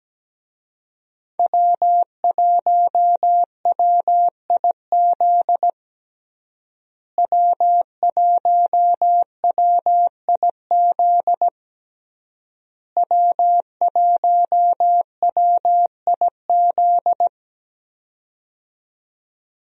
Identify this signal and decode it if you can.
Morse code, heard as a keyed tone, about 17 wpm, 700 Hz. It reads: W1WIZ W1WIZ W1WIZ
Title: USA Call Signs - 5 Characters - Repeated 3x - 17wpm